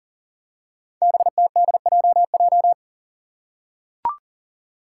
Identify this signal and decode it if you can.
Morse code, heard as a keyed tone, about 39 wpm, 700 Hz. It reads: BTDJJ
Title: Five Challenging Letter-Number Combinations 40wpm